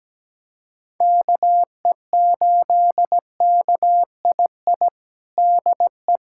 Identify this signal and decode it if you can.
Morse code, heard as a keyed tone, about 17 wpm, 700 Hz. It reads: KE8KII DE
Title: USA Callsign DE Callsign - 6 Characters - 17wpm